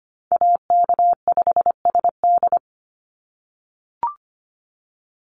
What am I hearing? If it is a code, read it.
Morse code, heard as a keyed tone, about 25 wpm, 700 Hz. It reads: AK5SD
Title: NA SOTA Chaser Callsigns 25wpm